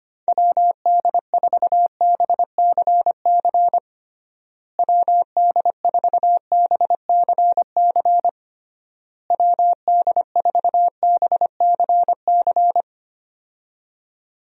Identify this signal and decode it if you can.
Morse code, heard as a keyed tone, about 25 wpm, 700 Hz. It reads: WD4BCC WD4BCC WD4BCC
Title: USA Call Signs - 6 Characters - Repeated 3x - 25wpm